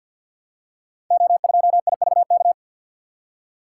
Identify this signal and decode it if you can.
Morse code, heard as a keyed tone, about 50 wpm, 700 Hz. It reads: O2IVK
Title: Code Groups - 5 Letter:Numbers - 50wpm